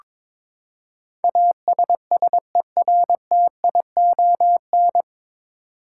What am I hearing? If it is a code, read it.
Morse code, heard as a keyed tone, about 22 wpm, 700 Hz. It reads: ASSERTION
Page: Morse Code Ninja practice 22wpm